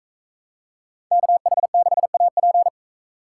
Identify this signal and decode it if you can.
Morse code, heard as a keyed tone, about 40 wpm, 700 Hz. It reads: KH6AP